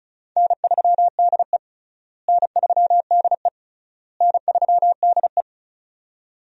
Morse code, heard as a keyed tone, about 36 wpm, 700 Hz. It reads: N3DE N3DE N3DE
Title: USA Call Signs - 4 Characters - Repeated 3x - 35wpm